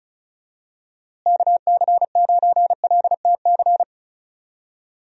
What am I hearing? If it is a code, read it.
Morse code, heard as a keyed tone, about 36 wpm, 700 Hz. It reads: KC9LTC
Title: USA Call Signs - 6 Characters - Repeated 2x - 35wpm